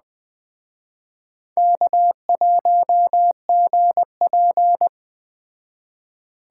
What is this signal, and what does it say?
Morse code, heard as a keyed tone, about 21 wpm, 700 Hz. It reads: K1GP